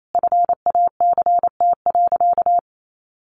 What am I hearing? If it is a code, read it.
Morse code, heard as a keyed tone, about 28 wpm, 700 Hz. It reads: FACT.
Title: Sentences from Top 1300 Words 28wpm